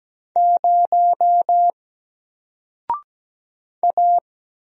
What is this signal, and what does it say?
Morse code, heard as a keyed tone, about 17 wpm, 700 Hz. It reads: 0 A
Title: Single Letter-Number 17wpm